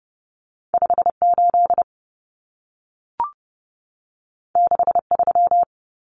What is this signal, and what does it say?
Morse code, heard as a keyed tone, about 30 wpm, 700 Hz. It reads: 58 63
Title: Common Cut Numbers - 2-Digit 30wpm